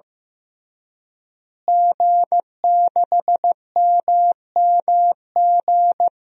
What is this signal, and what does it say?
Morse code, heard as a keyed tone, about 15 wpm, 700 Hz. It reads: G6MMG
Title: DXCC-100 Call Signs - 5 Characters - Repeated 3x - 15wpm